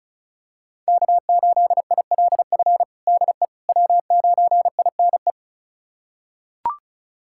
Morse code, heard as a keyed tone, about 36 wpm, 700 Hz. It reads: K8ILF DE W9INE
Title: USA Callsign DE Callsign - 5 Characters - 35wpm